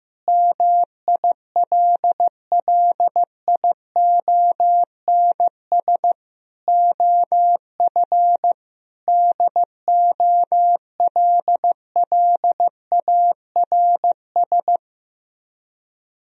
Morse code, heard as a keyed tone, about 15 wpm, 700 Hz. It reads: MILLIONS OF DOLLARS
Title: Speed-Racing - Sets of 5 Words 15wpm